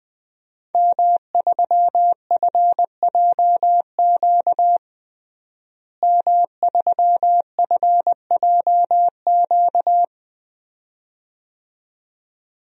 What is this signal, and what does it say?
Morse code, heard as a keyed tone, about 20 wpm, 700 Hz. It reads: M3FJQ M3FJQ
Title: DXCC-100 Call Signs - 5 Characters - Repeated 2x - 20wpm